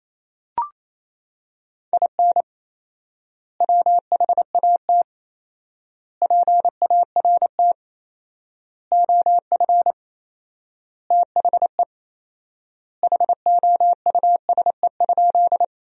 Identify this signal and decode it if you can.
Morse code, heard as a keyed tone, about 27 wpm, 700 Hz. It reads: IN WHAT PART OF THE HOUSE?
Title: 4x Word-Spacing - Sentences from Top 200 Words - 28wpm